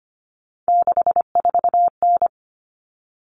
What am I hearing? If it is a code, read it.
Morse code, heard as a keyed tone, about 25 wpm, 700 Hz. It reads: 64N
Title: Common Cut Numbers - 3-Digit 25wpm